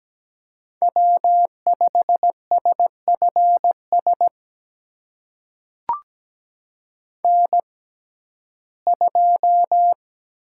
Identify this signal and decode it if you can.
Morse code, heard as a keyed tone, about 17 wpm, spaced as for 12 wpm, 700 Hz. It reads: W5SFS N2
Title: USA Call Signs - 5 Characters - 6x Character Spacing - 17wpm